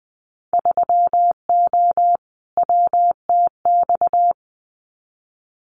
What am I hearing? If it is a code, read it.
Morse code, heard as a keyed tone, about 20 wpm, 700 Hz. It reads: 3O WTX
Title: Winter Field Day Exchanges Encore 20wpm